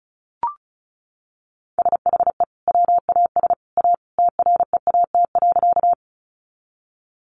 Morse code, heard as a keyed tone, about 34 wpm, 700 Hz. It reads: SHE WAS A TREAT.
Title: Sentences from Top 1300 Words 35wpm